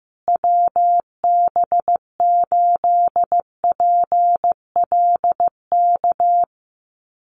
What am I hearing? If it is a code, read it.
Morse code, heard as a keyed tone, about 15 wpm, 700 Hz. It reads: WB8PLK
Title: USA Call Signs Repeated 2x 15wpm